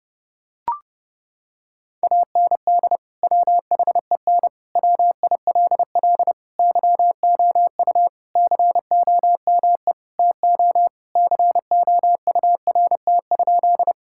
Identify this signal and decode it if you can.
Morse code, heard as a keyed tone, about 30 wpm, 700 Hz. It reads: AND WHEN WILL YOU COME TO COURT?
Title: Sentences from Top 1600 Words 30wpm